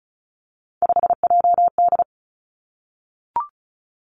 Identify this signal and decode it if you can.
Morse code, heard as a keyed tone, about 36 wpm, 700 Hz. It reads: 5JD
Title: Three Challenging Letter-Number Combinations 35wpm